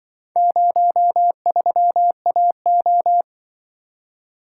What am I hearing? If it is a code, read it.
Morse code, heard as a keyed tone, about 24 wpm, 700 Hz. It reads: 03AO
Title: Speed-Racing - Four Letter-Number Combinations - 20wpm